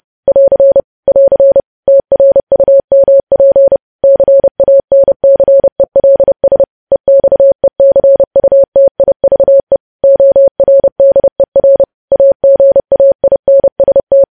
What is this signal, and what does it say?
Morse code, heard as a keyed tone, about 30 wpm, 550 Hz. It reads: <AR> <AR> TRUMP CANCELS EXECUTIVE ORDER AGAINST
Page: news headlines in morse code at 30 wpm